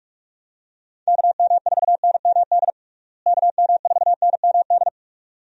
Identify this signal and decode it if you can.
Morse code, heard as a keyed tone, about 46 wpm, 700 Hz. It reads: KM4NMD KM4NMD
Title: USA Call Signs - 6 Characters - Repeated 2x - 45wpm